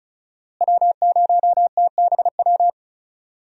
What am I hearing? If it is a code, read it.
Morse code, heard as a keyed tone, about 36 wpm, 700 Hz. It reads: W0TBW
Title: USA Call Signs - 5 Characters - Repeated 3x - 35wpm